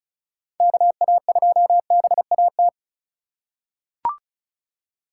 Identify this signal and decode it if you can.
Morse code, heard as a keyed tone, about 36 wpm, 700 Hz. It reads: KA2BAT
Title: USA Call Signs - 6 Characters - 4x Character Spacing - 35wpm